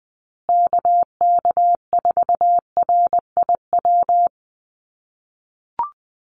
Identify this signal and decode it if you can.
Morse code, heard as a keyed tone, about 20 wpm, 700 Hz. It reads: KK4RIW
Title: USA Call Signs Repeated 3x 20wpm